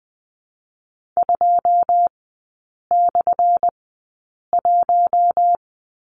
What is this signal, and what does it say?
Morse code, heard as a keyed tone, about 19 wpm, spaced as for 8 wpm, 700 Hz. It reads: 2/1
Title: 191 Lesson 31e - Letter v - Three Characters 20wpm